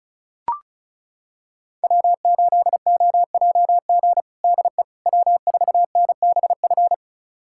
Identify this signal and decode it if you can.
Morse code, heard as a keyed tone, about 36 wpm, 700 Hz. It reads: W8OJG DE W4NBF
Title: USA Callsign DE Callsign - 5 Characters - 35wpm